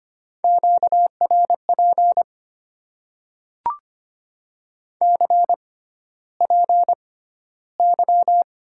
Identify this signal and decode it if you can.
Morse code, heard as a keyed tone, about 25 wpm, spaced as for 13 wpm, 690 Hz. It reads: QRP CPY
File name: Top 90 QSO Elements - 6x Character Spacing - 25wpm